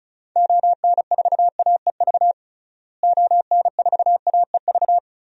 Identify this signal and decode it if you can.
Morse code, heard as a keyed tone, about 34 wpm, 700 Hz. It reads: ON4AEV ON4AEV
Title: DXCC-100 Call Signs - 6 Characters - Repeated 3x - 35wpm